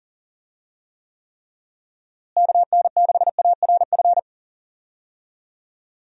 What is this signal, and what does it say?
Morse code, heard as a keyed tone, about 37 wpm, 700 Hz. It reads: KN6ARF
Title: USA Call Signs - 6 Characters - Repeated 3x - 40wpm